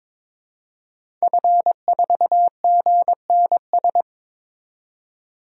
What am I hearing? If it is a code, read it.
Morse code, heard as a keyed tone, about 21 wpm, 700 Hz. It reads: F4GNS
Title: DXCC-100 Call Signs - 5 Characters - 22wpm